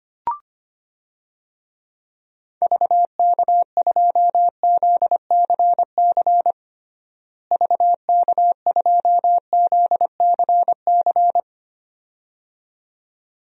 Morse code, heard as a keyed tone, about 25 wpm, 700 Hz. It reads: VK2ZCC VK2ZCC
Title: DXCC-100 Call Signs - 6 Characters - Repeated 2x - 25wpm